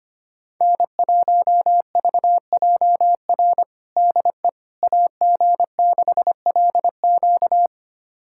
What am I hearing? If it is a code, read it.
Morse code, heard as a keyed tone, about 25 wpm, 700 Hz. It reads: N1VJR DE AG6LQ
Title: USA Callsign DE Callsign - 5 Characters - 25wpm